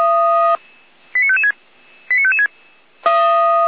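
Five-tone selective call (ZVEI), heard as quick runs of five tones, tones of about 70 ms. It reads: 89556 89556
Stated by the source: Five-tone ZVEI selective call received by SDR, served from audio.f85360.feuerwehr-orlamuende.org (fire brigade Orlamünde).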